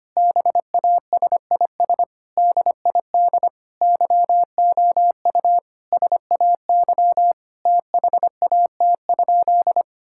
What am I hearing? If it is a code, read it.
Morse code, heard as a keyed tone, about 25 wpm, 700 Hz. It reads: BASIS DID YOU SAY THAT?
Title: Sentences from Top 1000 Words - Review Part 5 - 25wpm